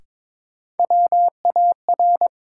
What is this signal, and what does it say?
Morse code, heard as a keyed tone, about 22 wpm, 700 Hz. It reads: WAR
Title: Sets of 5 Words - New Word Drill - 22wpm